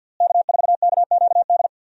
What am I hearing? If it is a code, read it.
Morse code, heard as a keyed tone, about 47 wpm, 700 Hz. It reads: K4KQD
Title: USA Call Signs - 5 Characters - Repeated 3x - 50wpm